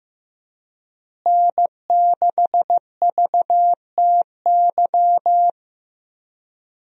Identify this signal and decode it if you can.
Morse code, heard as a keyed tone, about 16 wpm, 700 Hz. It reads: N6VTY
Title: USA Call Signs - 5 Characters - Repeated 3x - 15wpm